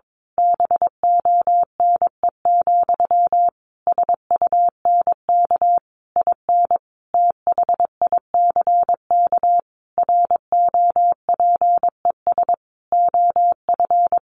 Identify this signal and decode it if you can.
Morse code, heard as a keyed tone, about 22 wpm, 700 Hz. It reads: BONE, SUNK IN THICK ROPES OF
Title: A Sound Of Thunder 22wpm